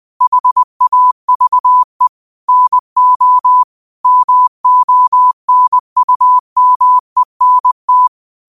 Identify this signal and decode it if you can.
Morse code, heard as a keyed tone, about 20 wpm, 1000 Hz. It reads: HAVE NO MONUMENT
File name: qotdfiles / 20wpm